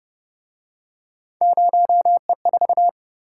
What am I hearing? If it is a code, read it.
Morse code, heard as a keyed tone, about 30 wpm, 700 Hz. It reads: 0E4